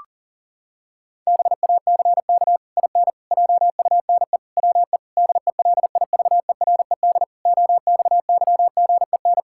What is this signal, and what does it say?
Morse code, heard as a keyed tone, about 40 wpm, 700 Hz. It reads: BACK IN JUNE WE DELIVERED OXYGEN
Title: Pangram Challenge v2 - 40wpm